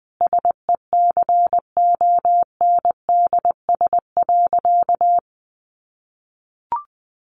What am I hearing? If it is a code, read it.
Morse code, heard as a keyed tone, about 20 wpm, 700 Hz. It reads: SECONDS.
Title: A Sound Of Thunder 20wpm